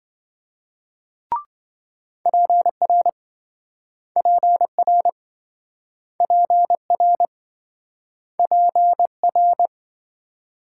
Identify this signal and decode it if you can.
Morse code, heard as a keyed tone, about 28 wpm, 700 Hz. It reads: PR PR PR PR